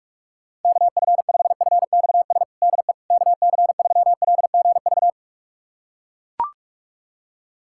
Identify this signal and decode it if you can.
Morse code, heard as a keyed tone, about 41 wpm, 700 Hz. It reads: KF5FXS DE KC3LGV
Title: USA Callsign DE Callsign - 6 Characters - 45wpm